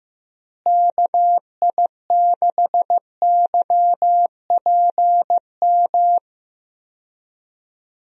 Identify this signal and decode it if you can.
Morse code, heard as a keyed tone, about 15 wpm, 700 Hz. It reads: KI6YPM
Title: USA Call Signs - 6 Characters - Repeated 3x - 15wpm